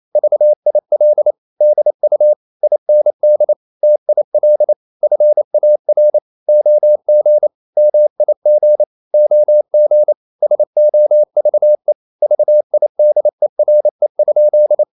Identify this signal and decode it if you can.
Morse code, heard as a keyed tone, about 28 wpm, 600 Hz. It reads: VIL DU IND TIL FAR OG MIG OG SOVE VIDERE?
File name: Ferie QRQ 28wpm